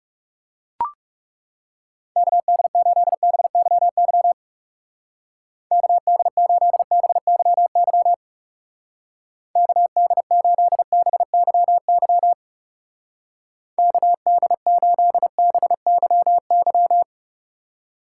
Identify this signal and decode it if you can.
Morse code, heard as a keyed tone, about 36 wpm, 700 Hz. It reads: KD8BYY KD8BYY KD8BYY KD8BYY
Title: Speed Racing - USA 2x3 Call Signs 30wpm